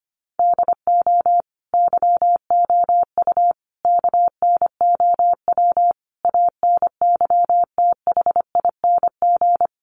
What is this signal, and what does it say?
Morse code, heard as a keyed tone, about 25 wpm, 700 Hz. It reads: DO YOU KNOW ANYTHING
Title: Sentences from Top 1000 Words 25wpm